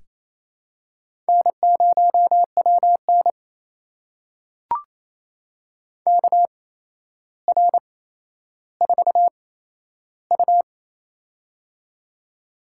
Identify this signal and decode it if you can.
Morse code, heard as a keyed tone, about 28 wpm, spaced as for 12 wpm, 700 Hz. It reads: N0WN KR4U
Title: USA Call Signs - 4 Characters - 8x Character Spacing - 28wpm